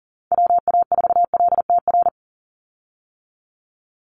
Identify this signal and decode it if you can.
Morse code, heard as a keyed tone, about 40 wpm, 700 Hz. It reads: WA4LTR
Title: USA Call Signs 40wpm